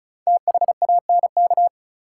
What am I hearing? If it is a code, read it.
Morse code, heard as a keyed tone, about 34 wpm, 700 Hz. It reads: THANK
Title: Sentences from Top 600 Words - New Word Drill - 35wpm